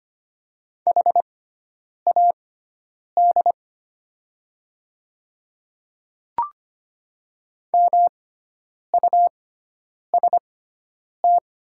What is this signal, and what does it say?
Morse code, heard as a keyed tone, about 25 wpm, spaced as for 9 wpm, 700 Hz. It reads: HAD MUST
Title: 6x Character Spacing - Top 300 Words - 25wpm